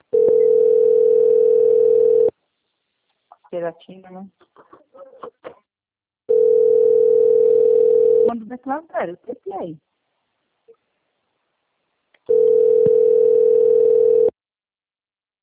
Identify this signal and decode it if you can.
Ringback tone (North America), 440 + 480 Hz, with speech.
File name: out- - - - - .14058